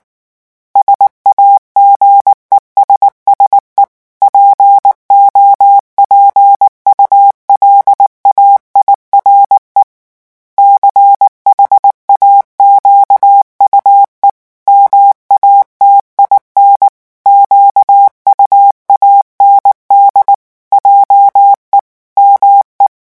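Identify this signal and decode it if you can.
Morse code, heard as a keyed tone, about 19 wpm, 800 Hz. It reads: SAGESSE POPULAIRE CHAQUE MATIN QUAND JE ME
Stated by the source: French Morse revision 8 drill of random five-letter groups at 18 words per minute.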